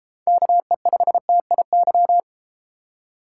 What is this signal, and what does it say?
Morse code, heard as a keyed tone, about 32 wpm, 700 Hz. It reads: KE5TIY